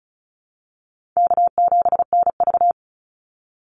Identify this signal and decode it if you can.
Morse code, heard as a keyed tone, about 34 wpm, 700 Hz. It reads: K7NV
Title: Easy Call Signs Repeated 3x - 35wpm